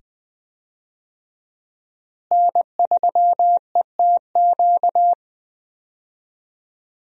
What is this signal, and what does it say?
Morse code, heard as a keyed tone, about 20 wpm, 700 Hz. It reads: N3ETQ
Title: USA Call Signs - 5 Characters - Repeated 2x - 20wpm